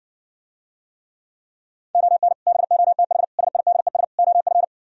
Morse code, heard as a keyed tone, about 60 wpm, 700 Hz. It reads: ON BOTH SIDES OF